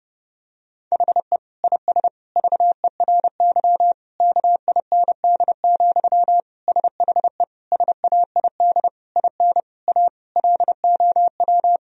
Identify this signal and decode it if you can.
Morse code, heard as a keyed tone, about 30 wpm, 700 Hz. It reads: HE IS VERY KIND, SHE SAID IN A LOW
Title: Sentences from Top 1000 Words - Review Part 1 - 30wpm